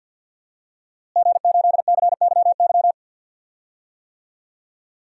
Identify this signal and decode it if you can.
Morse code, heard as a keyed tone, about 47 wpm, 700 Hz. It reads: G8CYY
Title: DXCC-100 Call Signs - 5 Characters - Repeated 3x - 50wpm